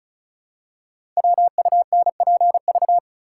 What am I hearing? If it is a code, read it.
Morse code, heard as a keyed tone, about 36 wpm, 700 Hz. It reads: WUNPV
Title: Code Groups - 5 Letters - 35wpm